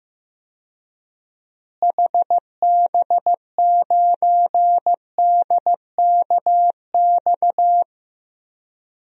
Morse code, heard as a keyed tone, about 15 wpm, 700 Hz. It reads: HB9DKX
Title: DXCC-100 Call Signs - 6 Characters - Repeated 3x - 15wpm